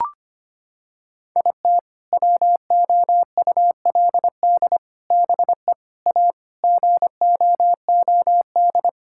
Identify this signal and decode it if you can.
Morse code, heard as a keyed tone, about 25 wpm, 700 Hz. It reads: IT WOULD BE A GOOD
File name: practice / 25wpm / Sentences with 3-Word Phrases - 25wpm